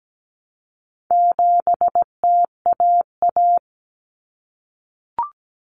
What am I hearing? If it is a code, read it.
Morse code, heard as a keyed tone, about 17 wpm, 700 Hz. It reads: 7TAA
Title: Common Cut Numbers - 4-Digit 17wpm